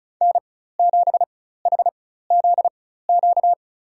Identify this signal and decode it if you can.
Morse code, heard as a keyed tone, about 36 wpm, spaced as for 16 wpm, 700 Hz. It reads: N7HZQ
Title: USA Call Signs - 4x Character Spacing - 35wpm